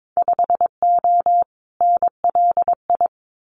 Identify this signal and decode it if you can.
Morse code, heard as a keyed tone, about 22 wpm, 700 Hz. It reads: 5O NLI
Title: Winter Field Day Exchanges 22wpm